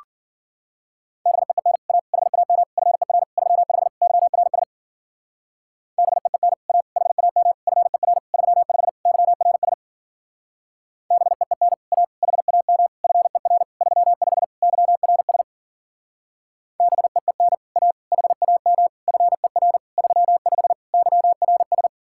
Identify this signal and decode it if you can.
Morse code, heard as a keyed tone, about 50 wpm, 700 Hz. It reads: BEEN A HAM FER 35 YRS BEEN A HAM FER 35 YRS BEEN A HAM FER 35 YRS BEEN A HAM FER 35 YRS
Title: Speed-Racing Common QSO Phrases - 40wpm